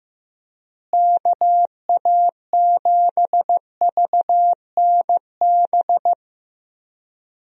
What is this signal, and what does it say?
Morse code, heard as a keyed tone, about 15 wpm, 700 Hz. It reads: KA7VNB
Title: USA Call Signs - 6 Characters - Repeated 3x - 15wpm